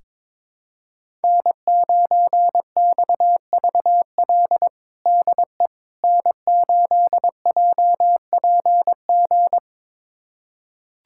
Morse code, heard as a keyed tone, about 22 wpm, 700 Hz. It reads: N9XVL DE N8JPG
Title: USA Callsign DE Callsign - 5 Characters - 22wpm